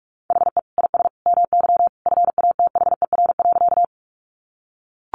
Morse code, heard as a keyed tone, about 44 wpm, 700 Hz. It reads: HE IS MY FATHER.
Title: Sentences from Top 300 Words 45wpm